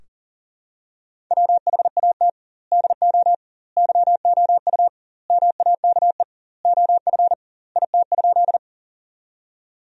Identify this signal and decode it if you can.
Morse code, heard as a keyed tone, about 40 wpm, 700 Hz. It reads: WHAT DO YOU MAKE OF IT?
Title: 2x Word-Spacing - Sentences from Top 100 Words - 40wpm